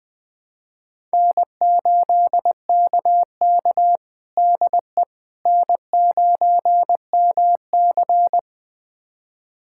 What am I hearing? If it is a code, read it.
Morse code, heard as a keyed tone, about 20 wpm, 700 Hz. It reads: N8KK DE N9MC
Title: USA Callsign DE Callsign - 4 Characters - 20wpm